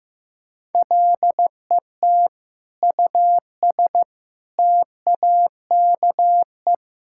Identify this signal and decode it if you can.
Morse code, heard as a keyed tone, about 15 wpm, 700 Hz. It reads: LET US TAKE
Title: Sentences from Top 1000 Words - Review Part 6 - 15wpm